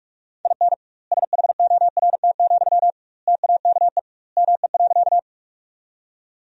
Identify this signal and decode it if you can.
Morse code, heard as a keyed tone, about 46 wpm, 700 Hz. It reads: IN SHORT, TAKE ME.
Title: 2x Word-Spacing - Sentences from Top 400 Words - 45wpm